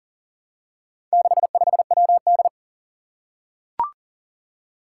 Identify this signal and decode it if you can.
Morse code, heard as a keyed tone, about 42 wpm, 700 Hz. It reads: 65WD
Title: Four Challenging Letter-Number Combinations 40wpm